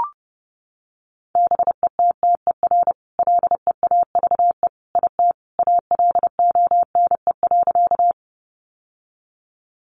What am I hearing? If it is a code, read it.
Morse code, heard as a keyed tone, about 28 wpm, 700 Hz. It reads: BETTER LEAVE IT ALONE.
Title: Sentences from Top 1000 Words 30wpm